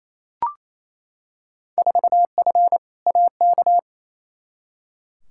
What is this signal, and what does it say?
Morse code, heard as a keyed tone, about 28 wpm, 700 Hz. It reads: 4F AK
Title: Field Day Exchanges Encore - 28wpm